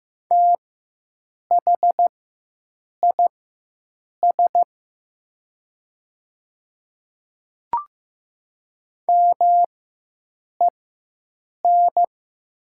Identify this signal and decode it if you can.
Morse code, heard as a keyed tone, about 15 wpm, spaced as for 7 wpm, 700 Hz. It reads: THIS MEN
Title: 4x Character Spacing - Top 200 Words - 15wpm